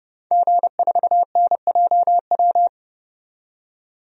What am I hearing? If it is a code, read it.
Morse code, heard as a keyed tone, about 30 wpm, 700 Hz. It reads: G4NJW
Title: DXCC-100 Call Signs - 5 Characters - 30wpm